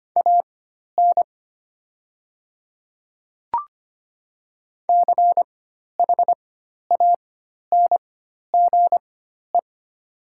Morse code, heard as a keyed tone, about 25 wpm, spaced as for 12 wpm, 700 Hz. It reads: AN CHANGE
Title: 4x Character Spacing - Top 200 Words - 25wpm